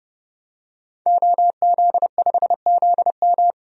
Morse code, heard as a keyed tone, about 32 wpm, 700 Hz. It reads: OZ5ZM
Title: DXCC-100 Call Signs - 5 Characters - Repeated 3x - 30wpm